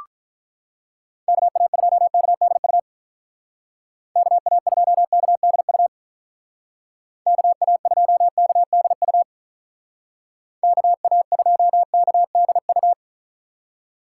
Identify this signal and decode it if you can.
Morse code, heard as a keyed tone, about 43 wpm, 700 Hz. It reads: KA2KDU KA2KDU KA2KDU KA2KDU
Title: Speed Racing - USA 2x3 Call Signs 35wpm